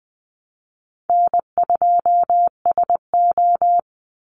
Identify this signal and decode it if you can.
Morse code, heard as a keyed tone, about 20 wpm, 700 Hz. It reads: N2SO
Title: Easy Call Signs Repeated 2x - 20wpm